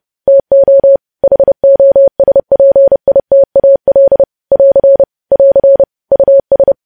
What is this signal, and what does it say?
Morse code, heard as a keyed tone, about 30 wpm, 550 Hz. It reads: TO HOSPITAL <AR> <AR> US
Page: News Headlines in Morse Code at 30 WPM